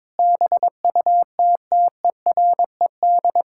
Morse code, heard as a keyed tone, about 22 wpm, 700 Hz. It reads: BUTTERED
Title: Pangram Challenge v2 - 22wpm